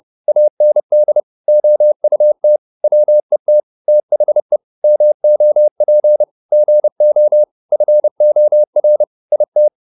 Morse code, heard as a keyed tone, about 30 wpm, 600 Hz. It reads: AND OUT WET THE MOP GO FOR IT